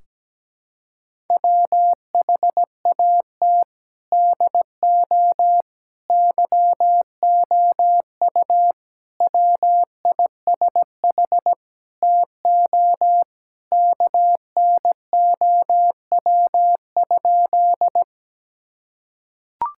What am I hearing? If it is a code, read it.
Morse code, heard as a keyed tone, about 17 wpm, 700 Hz. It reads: WHAT DO YOU WISH TO KNOW?
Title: Sentences from Top 1600 Words 17wpm